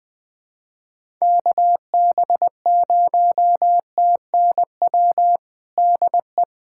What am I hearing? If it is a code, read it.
Morse code, heard as a keyed tone, about 21 wpm, 700 Hz. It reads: KB0TNW DE
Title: USA Callsign DE Callsign - 6 Characters - 20wpm